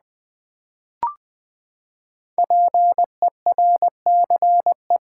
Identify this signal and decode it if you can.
Morse code, heard as a keyed tone, about 20 wpm, 700 Hz. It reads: PERCE
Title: Build A Word - 5 Letters At A Time - 10-Letter Words - 20wpm